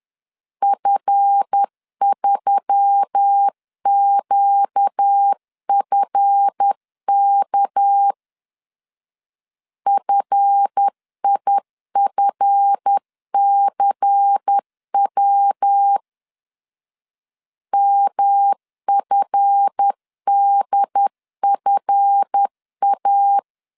Morse code, heard as a keyed tone, about 10 wpm, 800 Hz. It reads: F3QFK FIFCW MFDFA
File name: LEZIONI / 7 GF9Z / 7 GF 10wpm 2514